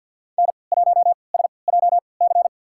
Morse code, heard as a keyed tone, about 46 wpm, spaced as for 27 wpm, 700 Hz. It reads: N1SJC
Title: USA Call Signs - 5 Characters - 3x Character Spacing - 50wpm